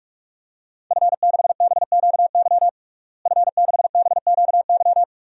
Morse code, heard as a keyed tone, about 41 wpm, 700 Hz. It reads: F6BQY F6BQY
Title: DXCC-100 Call Signs - 5 Characters - Repeated 3x - 45wpm